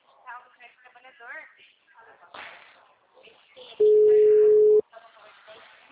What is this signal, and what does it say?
Ringback tone (Europe, 425 Hz), with speech.